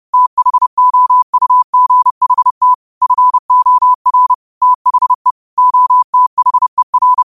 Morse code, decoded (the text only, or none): THOUGHT FOR THE OTHER